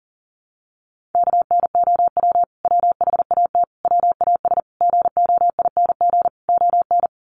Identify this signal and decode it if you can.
Morse code, heard as a keyed tone, about 40 wpm, 700 Hz. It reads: KNOW WHAT WAS GOING ON
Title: Sets of 5 Words Encore2 40wpm